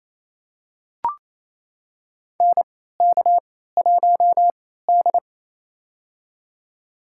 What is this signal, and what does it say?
Morse code, heard as a keyed tone, about 28 wpm, spaced as for 16 wpm, 700 Hz. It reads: NK1D
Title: USA Call Signs - 4 Characters - 3x Character Spacing - 28wpm